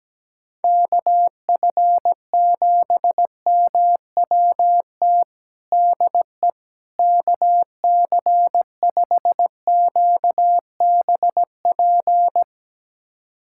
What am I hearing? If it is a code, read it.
Morse code, heard as a keyed tone, about 17 wpm, 700 Hz. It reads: KF7MWT DE KC5QBP
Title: USA Callsign DE Callsign - 6 Characters - 17wpm